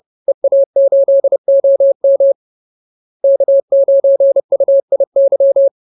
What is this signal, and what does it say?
Morse code, heard as a keyed tone, about 30 wpm, 550 Hz. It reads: EA8OM K9UIY
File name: CWT-213-30